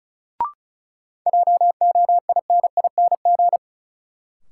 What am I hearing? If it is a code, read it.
Morse code, heard as a keyed tone, about 36 wpm, 700 Hz. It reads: JOINING
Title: Sets of 5 Words - New Word Drill - 35wpm